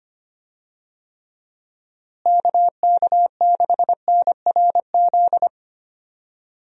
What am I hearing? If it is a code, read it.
Morse code, heard as a keyed tone, about 25 wpm, 700 Hz. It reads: KK6NRZ
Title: USA Call Signs - 6 Characters - Repeated 2x - 25wpm